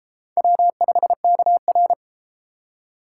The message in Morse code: W5KR

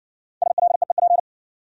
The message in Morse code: SLEEP

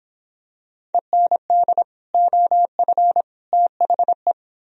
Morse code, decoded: END OF THE

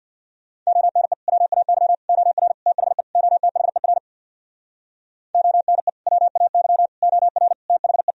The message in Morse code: ONE WAY OR THE OTHER ONE WAY OR THE

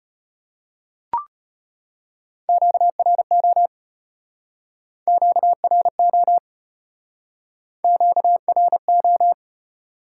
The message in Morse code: QRO QRO QRO

